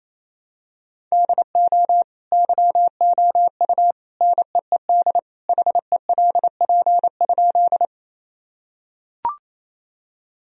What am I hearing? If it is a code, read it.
Morse code, heard as a keyed tone, about 28 wpm, 700 Hz. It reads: DO YOU NEED HELP?
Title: Sentences from Top 1000 Words - Review Part 1 - 28wpm